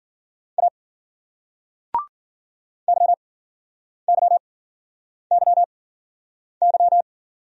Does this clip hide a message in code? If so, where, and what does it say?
Yes: Morse code from 0.6 to 7.0 s, heard as a keyed tone, about 55 wpm, spaced as for 9 wpm, 700 Hz. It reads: A YYYY